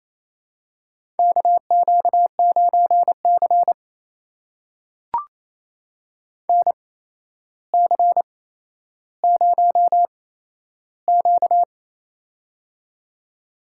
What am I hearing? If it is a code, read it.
Morse code, heard as a keyed tone, about 27 wpm, spaced as for 12 wpm, 700 Hz. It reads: KQ9C NC0Q
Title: USA Call Signs - 4 Characters - 8x Character Spacing - 28wpm